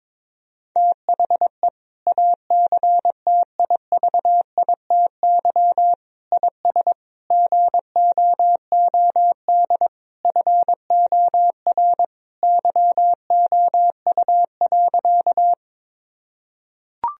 Morse code, heard as a keyed tone, about 22 wpm, 700 Hz. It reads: THE ACTIVITY IS GOOD FOR YOU.
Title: Sentences from Top 1000 Words - Review Part 4 - 22wpm